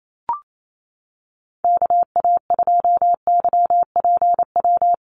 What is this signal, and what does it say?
Morse code, heard as a keyed tone, about 28 wpm, 700 Hz. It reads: KA2YPW
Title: USA Call Signs Repeated 3x 28wpm